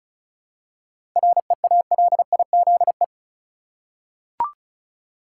Morse code, heard as a keyed tone, about 34 wpm, 700 Hz. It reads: REALIZE
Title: Seven Letter Words - 35wpm